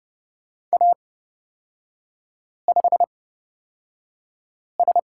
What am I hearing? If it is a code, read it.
Morse code, heard as a keyed tone, about 30 wpm, spaced as for 5 wpm, 700 Hz. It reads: A5S
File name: Lesson 36e - BT - Three Characters 30wpm